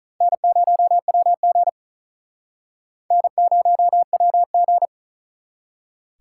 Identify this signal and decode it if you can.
Morse code, heard as a keyed tone, about 45 wpm, 700 Hz. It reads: N0WG N0WG